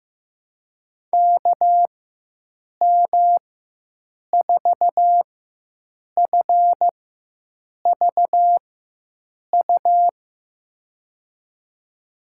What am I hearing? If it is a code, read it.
Morse code, heard as a keyed tone, about 15 wpm, spaced as for 7 wpm, 700 Hz. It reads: KM4FVU
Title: USA Call Signs - 4x Character Spacing - 15wpm